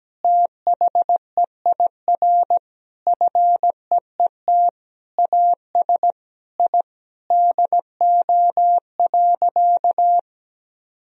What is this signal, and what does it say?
Morse code, heard as a keyed tone, about 17 wpm, 700 Hz. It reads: THEIR FEET AS I DO.
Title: Sentences from Top 1000 Words - Review Part 1 - 17wpm